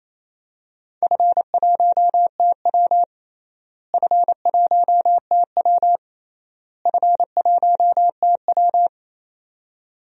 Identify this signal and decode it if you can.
Morse code, heard as a keyed tone, about 28 wpm, 700 Hz. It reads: F1TW F1TW F1TW
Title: DXCC-100 Call Signs - 4 Characters - Repeated 3x - 28wpm